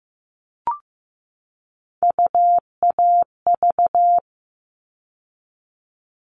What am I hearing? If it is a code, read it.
Morse code, heard as a keyed tone, about 15 wpm, 700 Hz. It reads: UAV
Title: All Cut Numbers - 3-Digit 15wpm